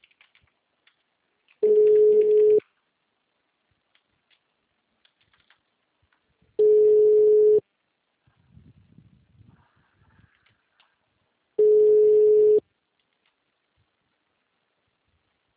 Ringback tone (Europe, 425 Hz).